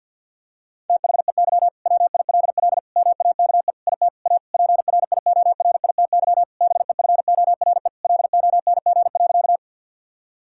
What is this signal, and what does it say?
Morse code, heard as a keyed tone, about 50 wpm, 700 Hz. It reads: THEY WILL MAKE IT A PRIORITY BEFORE LONG.
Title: Sentences from Top 1600 Words 50wpm